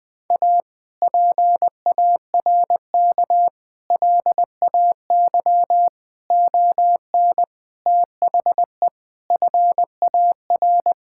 Morse code, heard as a keyed tone, about 20 wpm, 700 Hz. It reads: A PARK LAY ON THE FAR
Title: Sentences with 5-Word Phrases - 20wpm